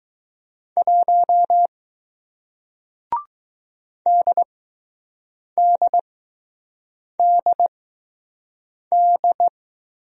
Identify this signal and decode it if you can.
Morse code, heard as a keyed tone, about 20 wpm, spaced as for 6 wpm, 700 Hz. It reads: 1 DDDD